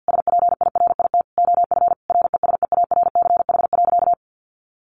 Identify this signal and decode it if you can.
Morse code, heard as a keyed tone, about 50 wpm, 700 Hz. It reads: SPIRIT OF RESEARCH.
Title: Sentences from Top 1400 Words 50wpm